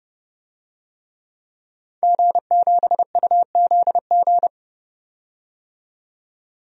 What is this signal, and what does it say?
Morse code, heard as a keyed tone, about 31 wpm, 700 Hz. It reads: G7UZG
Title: DXCC-100 Call Signs - 5 Characters - 30wpm